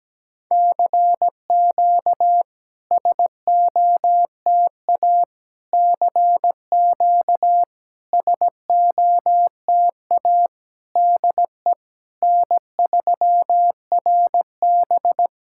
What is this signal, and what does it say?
Morse code, heard as a keyed tone, about 17 wpm, 700 Hz. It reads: CQ SOTA CQ SOTA DE N3RB
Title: SOTA QSO Practice 17wpm